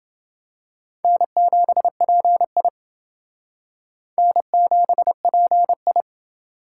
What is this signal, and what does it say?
Morse code, heard as a keyed tone, about 28 wpm, 700 Hz. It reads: N7PI N7PI